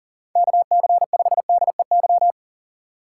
Morse code, heard as a keyed tone, about 39 wpm, 700 Hz. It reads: KC5DEY
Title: USA Call Signs - 6 Characters - 3x Character Spacing - 40wpm